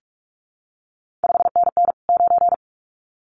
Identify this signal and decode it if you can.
Morse code, heard as a keyed tone, about 46 wpm, 700 Hz. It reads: 5NN 9
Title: CQ WW DX Contest Exchange Encore - 45wpm